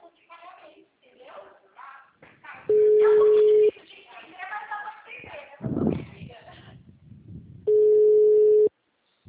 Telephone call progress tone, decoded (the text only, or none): ringback tone